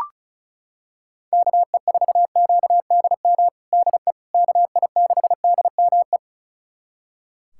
Morse code, heard as a keyed tone, about 34 wpm, 700 Hz. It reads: KE4QDM DE KI6DME
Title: USA Callsign DE Callsign - 6 Characters - 35wpm